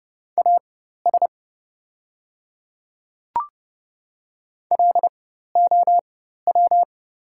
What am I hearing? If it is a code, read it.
Morse code, heard as a keyed tone, about 30 wpm, spaced as for 14 wpm, 700 Hz. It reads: AS LOW
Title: 4x Character Spacing - Top 200 Words - 30wpm